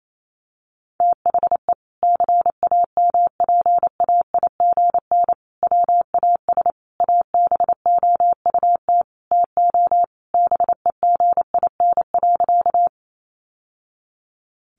Morse code, heard as a keyed tone, about 28 wpm, 700 Hz. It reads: THE CAMPAIGN WAS ABOUT TO BEGIN.
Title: Sentences from Top 1200 Words 28wpm